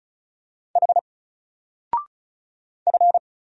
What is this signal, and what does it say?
Morse code, heard as a keyed tone, about 36 wpm, 700 Hz. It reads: H F
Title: Single Character (Letter:Number:Punctuation:Prosign) 35wpm